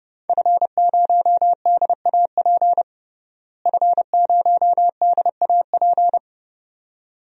Morse code, heard as a keyed tone, about 29 wpm, 700 Hz. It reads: F0DAP F0DAP
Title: DXCC-100 Call Signs - 5 Characters - Repeated 2x - 30wpm